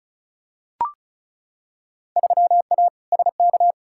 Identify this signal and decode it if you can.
Morse code, heard as a keyed tone, about 36 wpm, 700 Hz. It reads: 3A SK
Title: Field Day Exchanges Encore - 35wpm